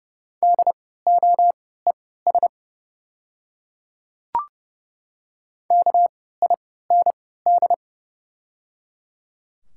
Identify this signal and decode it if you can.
Morse code, heard as a keyed tone, about 30 wpm, spaced as for 17 wpm, 700 Hz. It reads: DOES KIND
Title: 3x Character Spacing - Top 200 Words - 30wpm